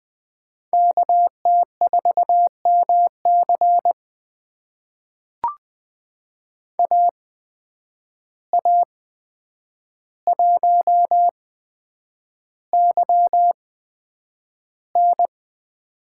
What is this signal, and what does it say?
Morse code, heard as a keyed tone, about 20 wpm, spaced as for 9 wpm, 700 Hz. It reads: KT4MC AA1YN
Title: USA Call Signs - 5 Characters - 8x Character Spacing - 20wpm